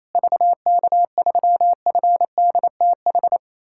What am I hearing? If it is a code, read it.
Morse code, heard as a keyed tone, about 28 wpm, 700 Hz. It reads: VK3FDTH
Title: USA Callsign DE Callsign 28wpm